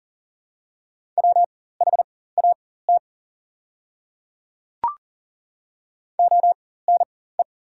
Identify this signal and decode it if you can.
Morse code, heard as a keyed tone, about 40 wpm, spaced as for 19 wpm, 690 Hz. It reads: WHAT ONE
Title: 4x Character Spacing - Top 100 Words - 40wpm